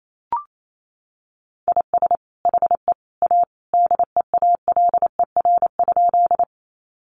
Morse code, heard as a keyed tone, about 28 wpm, 700 Hz. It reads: IS HE A DEALER?
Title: Sentences from Top 1100 Words 28wpm